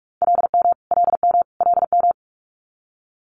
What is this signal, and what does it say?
Morse code, heard as a keyed tone, about 46 wpm, 700 Hz. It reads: LM LM LM
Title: Multi-letter Phonemes Repeated 3x 45wpm